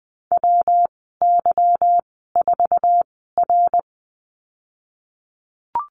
Morse code, heard as a keyed tone, about 20 wpm, spaced as for 15 wpm, 700 Hz. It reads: WY4R
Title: 173 Lesson 28g - Number 9 - Call Signs 20wpm